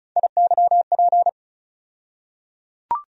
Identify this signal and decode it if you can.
Morse code, heard as a keyed tone, about 36 wpm, 700 Hz. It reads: IYP
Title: USA Call Sign Postfixes - 3 Letters 35wpm